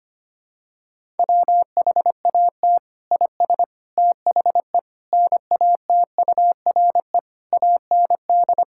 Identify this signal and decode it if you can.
Morse code, heard as a keyed tone, about 25 wpm, 700 Hz. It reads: WHAT IS THE NATURE AND
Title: Sentences from Top 1000 Words - Review Part 5 - 25wpm